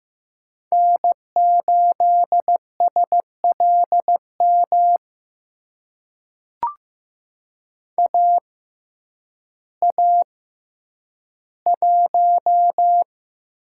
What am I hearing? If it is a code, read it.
Morse code, heard as a keyed tone, about 15 wpm, spaced as for 9 wpm, 700 Hz. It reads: N8SLM AA1
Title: USA Call Signs - 5 Characters - 6x Character Spacing - 15wpm